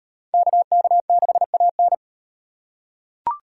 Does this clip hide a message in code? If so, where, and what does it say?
Yes: Morse code from 0.3 to 1.9 s, heard as a keyed tone, about 35 wpm, 700 Hz. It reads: KK6AN